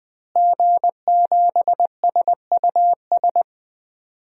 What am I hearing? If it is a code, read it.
Morse code, heard as a keyed tone, about 20 wpm, 700 Hz. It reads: G7SUS
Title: DXCC-100 Call Signs - 5 Characters - Repeated 3x - 20wpm